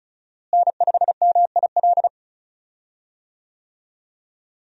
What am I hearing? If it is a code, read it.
Morse code, heard as a keyed tone, about 36 wpm, 700 Hz. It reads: N5MIL
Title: USA Call Signs - 5 Characters - Repeated 3x - 35wpm